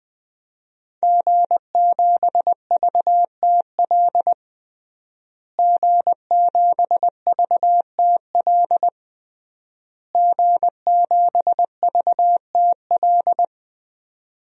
Morse code, heard as a keyed tone, about 20 wpm, 700 Hz. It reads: G7VTL G7VTL G7VTL
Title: DXCC-100 Call Signs - 5 Characters - Repeated 3x - 20wpm